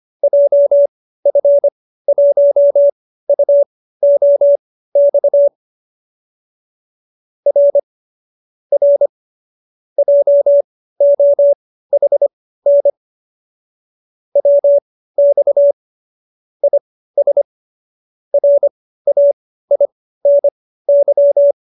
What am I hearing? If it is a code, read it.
Morse code, heard as a keyed tone, about 25 wpm, spaced as for 15 wpm, 570 Hz. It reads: JF1UOX R R JOHN WX IS RAINY